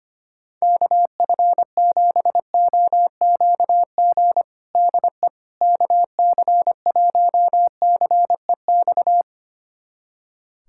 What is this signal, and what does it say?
Morse code, heard as a keyed tone, about 25 wpm, 700 Hz. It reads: KF7OQG DE KC1CEX
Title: USA Callsign DE Callsign - 6 Characters - 25wpm